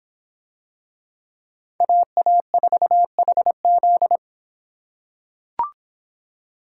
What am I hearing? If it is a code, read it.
Morse code, heard as a keyed tone, about 24 wpm, 700 Hz. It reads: AA4HZ